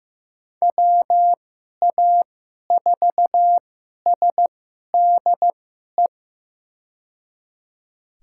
Morse code, heard as a keyed tone, about 15 wpm, spaced as for 11 wpm, 700 Hz. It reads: WA4SDE
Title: USA Call Signs - 6 Characters - 2x Character Spacing - 15wpm